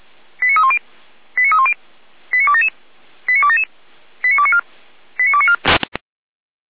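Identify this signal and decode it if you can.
Five-tone selective call (ZVEI), heard as quick runs of five tones, tones of about 70 ms. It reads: 89310 89310 89277 89277 89294 89294
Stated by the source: ZVEI five-tone selective call 89229 received via SDR.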